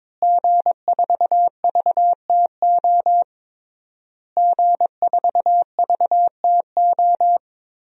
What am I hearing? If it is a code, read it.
Morse code, heard as a keyed tone, about 22 wpm, 700 Hz. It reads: G4VTO G4VTO
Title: DXCC-100 Call Signs - 5 Characters - Repeated 3x - 22wpm